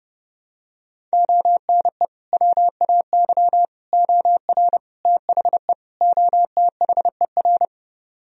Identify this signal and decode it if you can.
Morse code, heard as a keyed tone, about 32 wpm, 700 Hz. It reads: ONE WAY OR THE OTHER